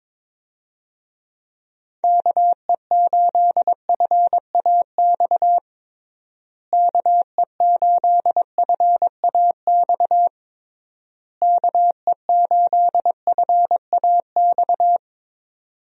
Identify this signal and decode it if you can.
Morse code, heard as a keyed tone, about 22 wpm, 700 Hz. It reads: KE8FAX KE8FAX KE8FAX
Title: USA Call Signs - 6 Characters - Repeated 3x - 22wpm